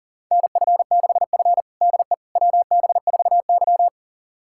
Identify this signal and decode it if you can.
Morse code, heard as a keyed tone, about 40 wpm, 700 Hz. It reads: NF6F DE WB4Y
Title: USA Callsign DE Callsign - 4 Characters - 40wpm